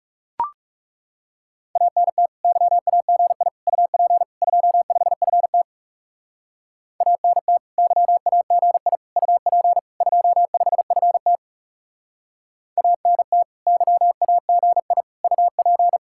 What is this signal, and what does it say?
Morse code, heard as a keyed tone, about 37 wpm, 700 Hz. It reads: ANT YAGI UP 25FT ANT YAGI UP 25FT ANT YAGI UP
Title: Speed-Racing Common QSO Phrases - 30wpm